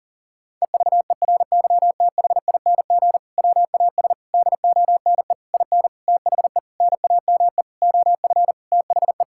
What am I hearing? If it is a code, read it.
Morse code, heard as a keyed tone, about 37 wpm, 700 Hz. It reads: EVERYTHING WAS DONE IN THE NAME OF THE
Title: Sentences from Top 1600 Words 40wpm